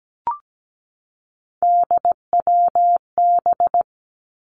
Morse code, heard as a keyed tone, about 17 wpm, 700 Hz. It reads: DWB
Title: Three Challenging Letter-Number Combinations 17wpm